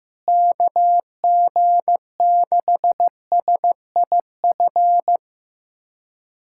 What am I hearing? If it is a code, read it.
Morse code, heard as a keyed tone, about 15 wpm, 700 Hz. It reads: KG6SIF
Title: USA Callsign DE Callsign 15wpm